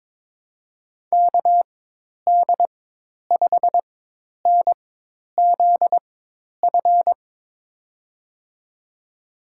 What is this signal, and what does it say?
Morse code, heard as a keyed tone, about 22 wpm, spaced as for 10 wpm, 700 Hz. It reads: KD5NZF
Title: USA Call Signs - 6 Characters - 4x Character Spacing - 22wpm